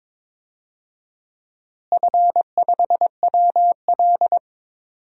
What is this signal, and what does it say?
Morse code, heard as a keyed tone, about 22 wpm, 700 Hz. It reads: F5WL
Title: DXCC-100 Call Signs - 4 Characters - Repeated 3x - 22wpm